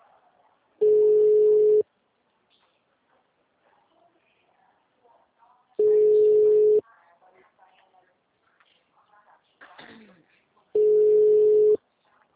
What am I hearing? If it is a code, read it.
Ringback tone (Europe, 425 Hz), with speech.